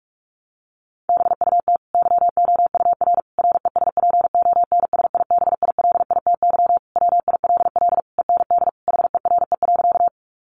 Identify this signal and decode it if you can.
Morse code, heard as a keyed tone, about 44 wpm, 700 Hz. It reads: BUT YOUR RESPONSIBILITY WILL END HERE.
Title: Sentences from Top 1200 Words 45wpm